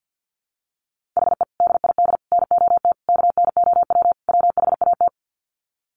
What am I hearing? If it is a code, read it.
Morse code, heard as a keyed tone, about 50 wpm, 700 Hz. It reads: HE DID NOT KNOW WHAT
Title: Sets of 5 Words Encore 50wpm